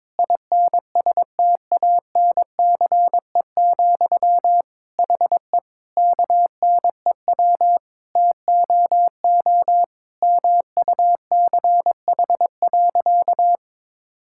Morse code, heard as a keyed tone, about 23 wpm, 700 Hz. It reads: INSTANCE, HE KNEW TOO MUCH.
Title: Sentences from Top 1000 Words - Review Part 5 - 22wpm